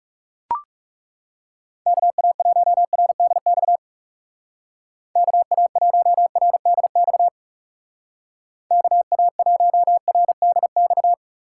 Morse code, heard as a keyed tone, about 36 wpm, 700 Hz. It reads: KA1RDX KA1RDX KA1RDX
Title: Speed Racing - USA 2x3 Call Signs 30wpm